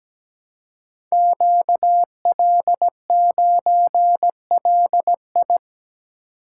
Morse code, heard as a keyed tone, about 17 wpm, 700 Hz. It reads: QL9LI
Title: Code Groups - 5 Letter:Numbers - 17wpm